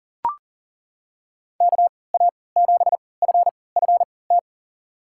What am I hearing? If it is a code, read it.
Morse code, heard as a keyed tone, about 40 wpm, spaced as for 23 wpm, 700 Hz. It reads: KA7FFT
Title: USA Call Signs - 3x Character Spacing - 40wpm